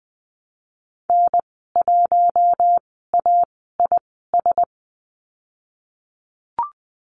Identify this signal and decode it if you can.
Morse code, heard as a keyed tone, about 20 wpm, spaced as for 15 wpm, 700 Hz. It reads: N1AIS
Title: 069 Lesson 12g - Letter d - Call Signs 20wpm